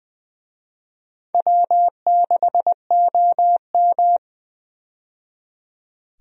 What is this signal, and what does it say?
Morse code, heard as a keyed tone, about 20 wpm, 700 Hz. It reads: W6OM
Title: USA Call Signs - 4 Characters - 20wpm